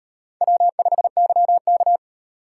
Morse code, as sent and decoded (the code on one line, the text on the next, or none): .-- ..... -.-- -.-
W5YK